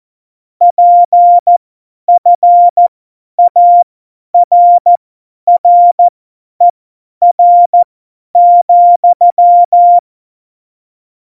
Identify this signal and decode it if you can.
Morse code, heard as a keyed tone, about 14 wpm, spaced as for 10 wpm, 700 Hz. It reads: PFARRER,
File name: Morsekurs / Witze / Witz 20. Oktober 2020, 50 F70